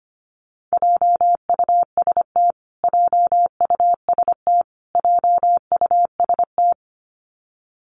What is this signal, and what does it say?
Morse code, heard as a keyed tone, about 25 wpm, 700 Hz. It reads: JUST JUST JUST
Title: Top 300 Words Repeated 25wpm